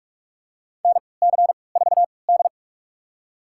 Morse code, heard as a keyed tone, about 46 wpm, spaced as for 25 wpm, 690 Hz. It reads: NC4D
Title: USA Call Signs - 4 Characters - 3x Character Spacing - 45wpm